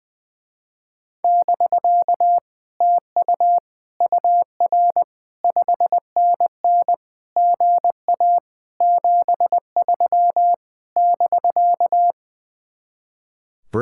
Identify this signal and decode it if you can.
Morse code, heard as a keyed tone, about 20 wpm, 700 Hz. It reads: <BK> TU UR 5NN GA 73 <BK>
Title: Speed-Racing Common QSO Phrases - 20wpm